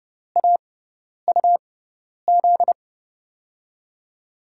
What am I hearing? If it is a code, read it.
Morse code, heard as a keyed tone, about 32 wpm, spaced as for 10 wpm, 700 Hz. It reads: AUZ
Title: USA Call Signs - 6x Character Spacing - 30wpm